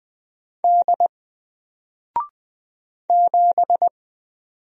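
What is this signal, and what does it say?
Morse code, heard as a keyed tone, about 20 wpm, 700 Hz. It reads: D 7
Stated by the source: Single letter and number Morse drill at 20 wpm, from N4LCD.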